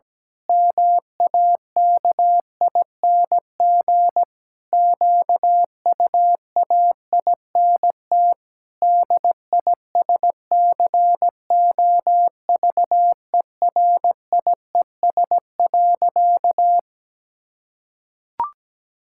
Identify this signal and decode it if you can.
Morse code, heard as a keyed tone, about 17 wpm, 700 Hz. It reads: MAKING QUAINT DISCOVERIES.
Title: Pangram Challenge v2 - 17wpm